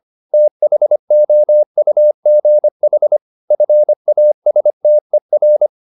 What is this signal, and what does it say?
Morse code, heard as a keyed tone, about 25 wpm, 600 Hz. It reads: THOUGH FASTER
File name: the old man and the sea part 1 25wpm